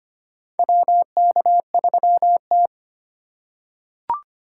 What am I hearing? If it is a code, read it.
Morse code, heard as a keyed tone, about 25 wpm, 700 Hz. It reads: WK3T
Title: USA Call Signs - 4 Characters - 2x Character Spacing - 25wpm